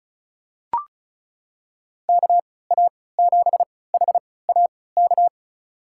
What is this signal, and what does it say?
Morse code, heard as a keyed tone, about 36 wpm, spaced as for 20 wpm, 700 Hz. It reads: KA7HAK
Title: USA Call Signs - 3x Character Spacing - 35wpm